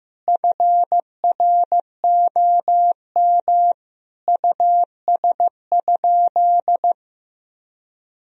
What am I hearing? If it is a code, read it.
Morse code, heard as a keyed tone, about 15 wpm, 700 Hz. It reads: FROM US?
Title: Sentences from Top 1000 Words - Review Part 4 - 15wpm